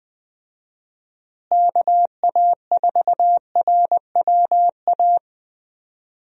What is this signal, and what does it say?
Morse code, heard as a keyed tone, about 20 wpm, 700 Hz. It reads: KA4RWA
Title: USA Call Signs - 6 Characters - Repeated 2x - 20wpm